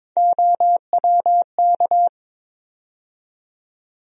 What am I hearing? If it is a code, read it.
Morse code, heard as a keyed tone, about 21 wpm, 700 Hz. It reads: OWK
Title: USA Call Sign Postfixes - 3 Letters 22wpm